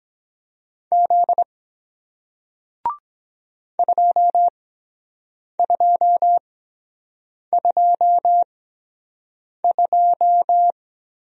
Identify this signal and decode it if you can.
Morse code, heard as a keyed tone, about 23 wpm, spaced as for 7 wpm, 700 Hz. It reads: Z 2222